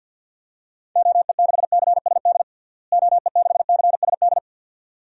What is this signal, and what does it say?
Morse code, heard as a keyed tone, about 53 wpm, 700 Hz. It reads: OE6CSD OE6CSD
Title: DXCC-100 Call Signs - 6 Characters - Repeated 2x - 50wpm